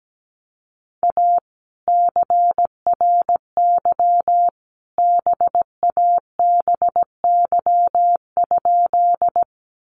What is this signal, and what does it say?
Morse code, heard as a keyed tone, about 17 wpm, 700 Hz. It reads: A CRY BABY?
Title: Sentences from Top 1300 Words 17wpm